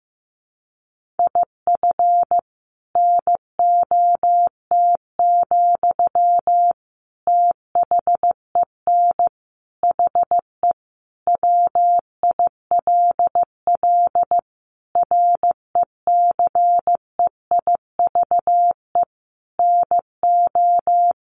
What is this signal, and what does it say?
Morse code, heard as a keyed tone, about 15 wpm, 700 Hz. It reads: IF NOT, THEN HE WILL RECEIVE NO
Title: Sentences from Top 1500 Words 15wpm